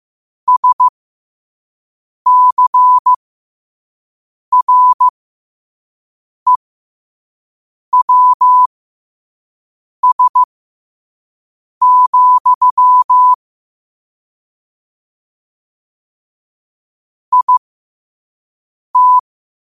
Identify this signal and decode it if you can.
Morse code, heard as a keyed tone, about 15 wpm, spaced as for 5 wpm, 1000 Hz. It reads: SCREWS, IT
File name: qotdfiles / 5wpm